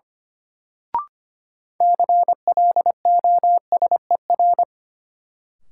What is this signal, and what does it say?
Morse code, heard as a keyed tone, about 25 wpm, 700 Hz. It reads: CLOSER
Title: Sets of 5 Words Encore - New Word Drill - 25wpm